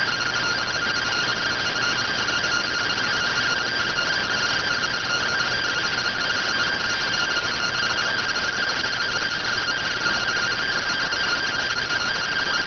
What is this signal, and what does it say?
Two-tone teleprinter signal, about 170 Hz shift.